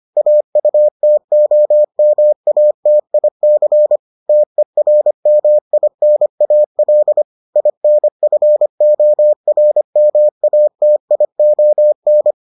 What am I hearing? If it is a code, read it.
Morse code, heard as a keyed tone, about 25 wpm, 600 Hz. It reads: AUTOMATIC TERMINAL INFORMATION